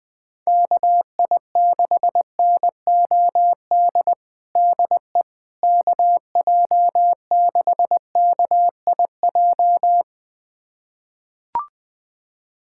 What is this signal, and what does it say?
Morse code, heard as a keyed tone, about 20 wpm, 700 Hz. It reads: KI6NOD DE KJ6KIJ
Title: USA Callsign DE Callsign - 6 Characters - 20wpm